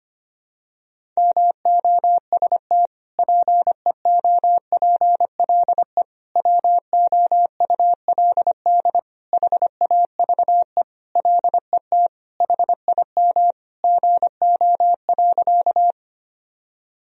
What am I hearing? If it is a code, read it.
Morse code, heard as a keyed tone, about 25 wpm, 700 Hz. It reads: MOST PEOPLE WOULD HAVE LET HIM GO.
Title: Sentences from Top 1000 Words - Review Part 1 - 25wpm